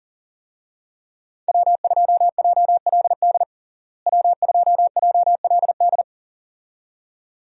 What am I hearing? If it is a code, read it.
Morse code, heard as a keyed tone, about 40 wpm, 700 Hz. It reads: W2JLD W2JLD
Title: USA Call Signs - 5 Characters - Repeated 2x - 40wpm